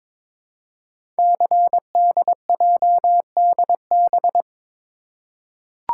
Morse code, heard as a keyed tone, about 22 wpm, 700 Hz. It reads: CDJDB
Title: Code Groups - 5 Letters - 22wpm